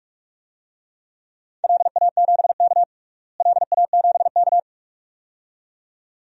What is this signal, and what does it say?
Morse code, heard as a keyed tone, about 45 wpm, 700 Hz. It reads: LA7K LA7K
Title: DXCC-100 Call Signs - 4 Characters - Repeated 2x - 45wpm